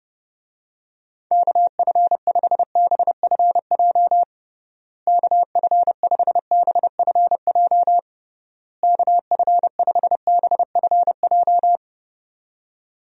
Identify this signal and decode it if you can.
Morse code, heard as a keyed tone, about 30 wpm, 700 Hz. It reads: KF5BFJ KF5BFJ KF5BFJ
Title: USA Call Signs - 6 Characters - Repeated 3x - 30wpm